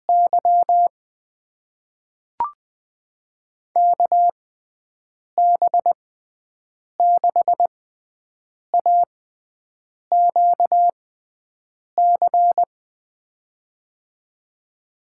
Morse code, heard as a keyed tone, about 20 wpm, spaced as for 7 wpm, 700 Hz. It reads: Y KB6AQC
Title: USA Call Signs - 6 Characters - 6x Character Spacing - 20wpm